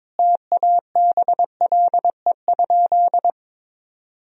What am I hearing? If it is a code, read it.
Morse code, heard as a keyed tone, about 22 wpm, 700 Hz. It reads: TABLE?
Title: Sentences from Top 1000 Words - Review Part 3 - 22wpm